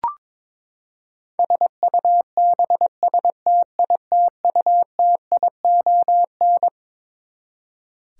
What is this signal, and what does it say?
Morse code, heard as a keyed tone, about 21 wpm, 700 Hz. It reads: SUBSTITUTION
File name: Twelve Letter Words - 22wpm